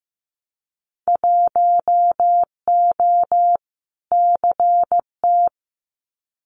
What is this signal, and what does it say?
Morse code, heard as a keyed tone, about 15 wpm, 700 Hz. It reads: 1O CT
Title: Winter Field Day Exchanges 15wpm